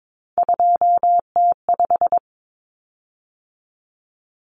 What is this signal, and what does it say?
Morse code, heard as a keyed tone, about 22 wpm, 700 Hz. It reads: 2T5
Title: Common Cut Numbers - 3-Digit 22wpm